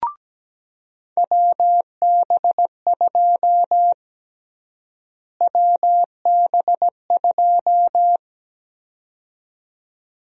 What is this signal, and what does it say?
Morse code, heard as a keyed tone, about 17 wpm, 700 Hz. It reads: WB2 WB2
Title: USA Call Sign Prefixes - Repeated 2x - 17wpm